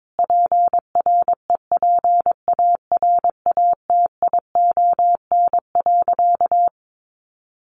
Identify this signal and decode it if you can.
Morse code, heard as a keyed tone, about 22 wpm, 700 Hz. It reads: PREPARATION.
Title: Sentences from Top 1400 Words 22wpm